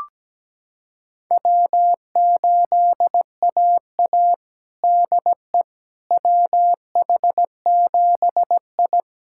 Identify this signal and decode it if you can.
Morse code, heard as a keyed tone, about 17 wpm, 700 Hz. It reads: W8AA DE WH7I
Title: USA Callsign DE Callsign - 4 Characters - 17wpm